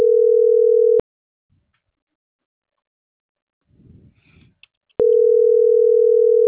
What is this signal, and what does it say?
Ringback tone (North America), 440 + 480 Hz.